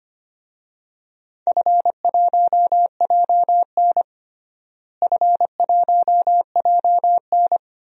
Morse code, heard as a keyed tone, about 25 wpm, 700 Hz. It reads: F1JN F1JN
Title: DXCC-100 Call Signs - 4 Characters - Repeated 3x - 25wpm